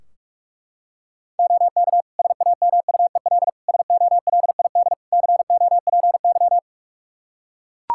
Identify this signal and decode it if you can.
Morse code, heard as a keyed tone, about 41 wpm, 700 Hz. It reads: OK SAMUEL SOLID COPY
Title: K1USN SST QSO Practice 45wpm